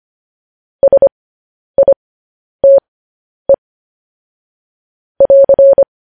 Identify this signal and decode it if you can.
Morse code, heard as a keyed tone, about 25 wpm, spaced as for 10 wpm, 550 Hz. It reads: SITE <AR>
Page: news headlines in morse code at 10 wpm